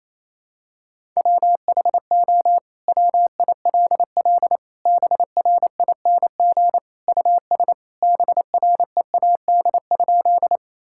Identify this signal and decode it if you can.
Morse code, heard as a keyed tone, about 28 wpm, 700 Hz. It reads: WHO WILL BRING US BREAD?
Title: Sentences from Top 1600 Words 28wpm